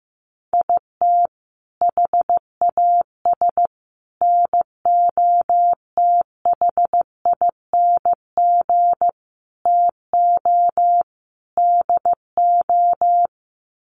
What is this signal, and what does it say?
Morse code, heard as a keyed tone, about 15 wpm, 700 Hz. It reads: IT HAS NOTHING TO DO
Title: Sets of 5 Words Encore 15wpm